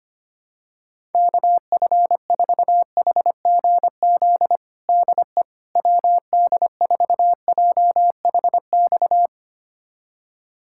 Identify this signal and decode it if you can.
Morse code, heard as a keyed tone, about 25 wpm, 700 Hz. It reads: KF4HGZ DE WD4JHX
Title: USA Callsign DE Callsign - 6 Characters - 25wpm